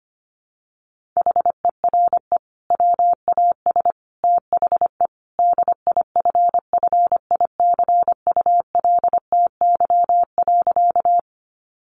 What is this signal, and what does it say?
Morse code, heard as a keyed tone, about 25 wpm, 700 Hz. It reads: HERE WAS THE DIFFICULTY.
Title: Sentences from Top 1500 Words 25wpm